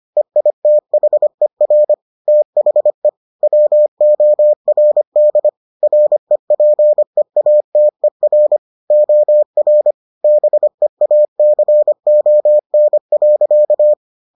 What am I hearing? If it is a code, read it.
Morse code, heard as a keyed tone, about 25 wpm, 600 Hz. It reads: EITHER THE WORD REPEATER OR BEACON.